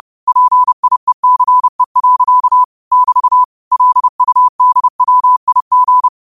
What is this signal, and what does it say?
Morse code, heard as a keyed tone, about 30 wpm, 1000 Hz. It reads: PIECE. <BT> LUDWIG
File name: qotdfiles / 30wpm